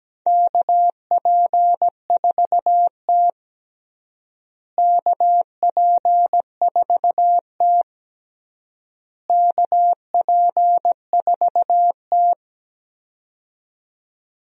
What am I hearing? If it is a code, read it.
Morse code, heard as a keyed tone, about 17 wpm, 700 Hz. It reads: KP4T KP4T KP4T
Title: DXCC-100 Call Signs - 4 Characters - Repeated 3x - 17wpm